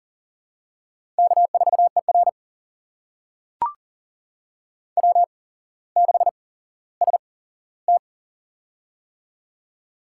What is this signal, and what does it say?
Morse code, heard as a keyed tone, about 42 wpm, spaced as for 17 wpm, 700 Hz. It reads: K4ER W6ST
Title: USA Call Signs - 4 Characters - 8x Character Spacing - 40wpm